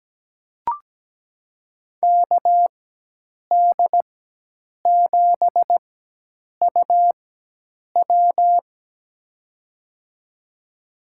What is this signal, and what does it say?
Morse code, heard as a keyed tone, about 17 wpm, spaced as for 8 wpm, 700 Hz. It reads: KD7UW
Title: USA Call Signs - 5 Characters - 4x Character Spacing - 17wpm